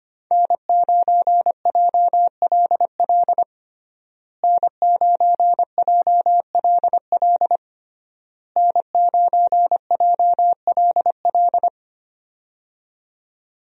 Morse code, heard as a keyed tone, about 25 wpm, 700 Hz. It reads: N9JLL N9JLL N9JLL
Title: USA Call Signs - 5 Characters - Repeated 3x - 25wpm